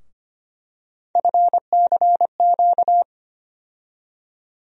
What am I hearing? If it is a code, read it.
Morse code, heard as a keyed tone, about 25 wpm, 700 Hz. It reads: FCQ
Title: USA Call Sign Postfixes - 3 Letters 25wpm